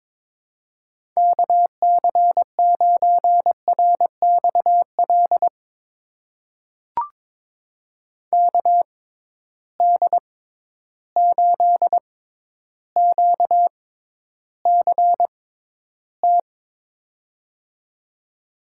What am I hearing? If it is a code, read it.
Morse code, heard as a keyed tone, about 22 wpm, spaced as for 11 wpm, 700 Hz. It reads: KC9RXL KD8QCT
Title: USA Call Signs - 6 Characters - 6x Character Spacing - 22wpm